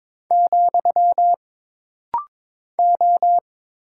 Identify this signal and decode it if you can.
Morse code, heard as a keyed tone, about 22 wpm, 700 Hz. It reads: , O
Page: This rapid-fire timing is geared to push your ICR (Instant Character Recognition) skills.